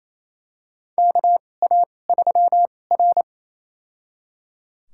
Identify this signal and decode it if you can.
Morse code, heard as a keyed tone, about 28 wpm, spaced as for 20 wpm, 700 Hz. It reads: KA3R
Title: USA Call Signs - 2x Character Spacing - 28wpm